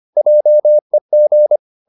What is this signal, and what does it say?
Morse code, heard as a keyed tone, about 25 wpm, 600 Hz. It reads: JEG
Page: Ferie 25 wpm